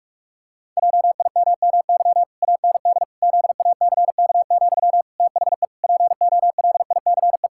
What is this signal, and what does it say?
Morse code, heard as a keyed tone, about 46 wpm, 700 Hz. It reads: JIMMY AND ZACK, THE POLICE
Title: Pangram Challenge v2 - 45wpm